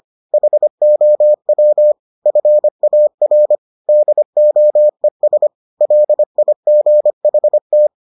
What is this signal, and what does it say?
Morse code, heard as a keyed tone, about 25 wpm, 600 Hz. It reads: HOW FAR DOES LIGHT